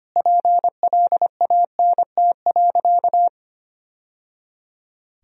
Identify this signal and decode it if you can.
Morse code, heard as a keyed tone, about 25 wpm, 700 Hz. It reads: PLANT.
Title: Sentences from Top 1000 Words - Review Part 1 - 25wpm